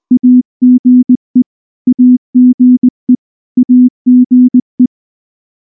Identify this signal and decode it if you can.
Morse code, heard as a keyed tone, about 19 wpm, 250 Hz. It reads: AGE AGE AGE
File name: morse / morse-training-20WPM